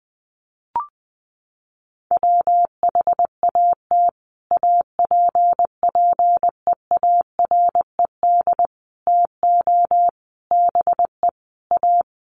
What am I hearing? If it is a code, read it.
Morse code, heard as a keyed tone, about 20 wpm, 700 Hz. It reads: WHAT APPEARED TO BE A
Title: Sets of 5 Words Encore2 20wpm